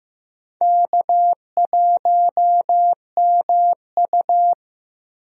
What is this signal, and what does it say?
Morse code, heard as a keyed tone, about 15 wpm, 700 Hz. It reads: K1MU
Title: USA Call Signs - 4 Characters - Repeated 3x - 15wpm